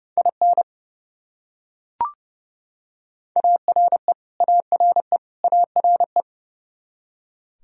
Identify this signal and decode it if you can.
Morse code, heard as a keyed tone, about 30 wpm, 700 Hz. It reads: IN ARE ARE ARE
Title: Top 100 Words Repeated 30wpm